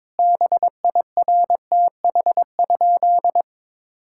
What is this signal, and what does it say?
Morse code, heard as a keyed tone, about 22 wpm, 700 Hz. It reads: BIRTH?
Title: Sentences from Top 1000 Words - Review Part 6 - 22wpm